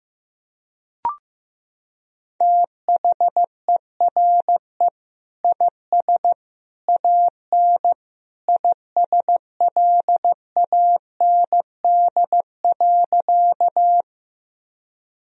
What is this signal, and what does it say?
Morse code, heard as a keyed tone, about 16 wpm, 700 Hz. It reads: THERE IS AN ISLAND.
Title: Sentences from Top 1000 Words - Review Part 3 - 15wpm